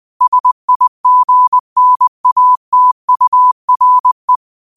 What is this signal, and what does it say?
Morse code, heard as a keyed tone, about 20 wpm, 1000 Hz. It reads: SIGNATURE